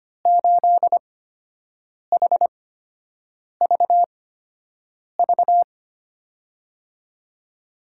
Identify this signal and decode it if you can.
Morse code, heard as a keyed tone, about 25 wpm, spaced as for 7 wpm, 700 Hz. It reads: 8HVV
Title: USA Call Signs - 6 Characters - 8x Character Spacing - 25wpm